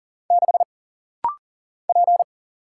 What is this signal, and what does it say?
Morse code, heard as a keyed tone, about 39 wpm, 700 Hz. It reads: 6 P
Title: Single Letter-Number - Rapid-Fire 40wpm